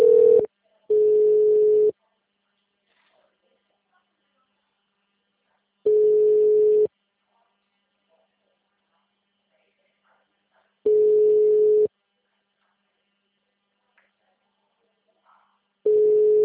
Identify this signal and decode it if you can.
Ringback tone (Europe, 425 Hz).
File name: out- -2002- - - .634